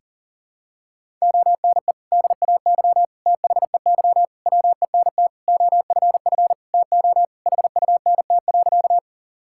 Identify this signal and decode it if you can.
Morse code, heard as a keyed tone, about 40 wpm, 700 Hz. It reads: ONE DAY THEY WENT OFF TO HUNT.
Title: Sentences from Top 1600 Words 40wpm